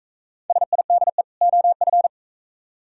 Morse code, heard as a keyed tone, about 43 wpm, 700 Hz. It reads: SIDE OF